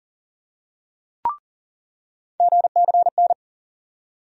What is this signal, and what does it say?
Morse code, heard as a keyed tone, about 40 wpm, 700 Hz. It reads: GCN
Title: USA Call Sign Postfixes - 3 Letters 40wpm